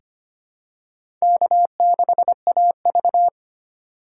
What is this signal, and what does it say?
Morse code, heard as a keyed tone, about 25 wpm, 700 Hz. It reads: K6AV
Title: USA Call Signs - 4 Characters - 3x Character Spacing - 25wpm